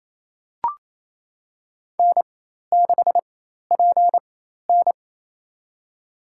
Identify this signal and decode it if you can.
Morse code, heard as a keyed tone, about 27 wpm, spaced as for 13 wpm, 700 Hz. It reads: N6PN
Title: USA Call Signs - 4 Characters - 4x Character Spacing - 28wpm